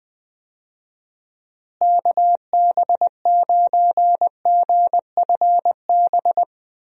Morse code, heard as a keyed tone, about 21 wpm, 700 Hz. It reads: KB9GFB
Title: USA Call Signs - 6 Characters - Repeated 2x - 20wpm